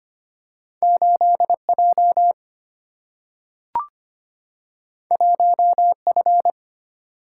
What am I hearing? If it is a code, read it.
Morse code, heard as a keyed tone, about 25 wpm, 700 Hz. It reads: 8 J 1 F
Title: Single-Letter Single-Number 25wpm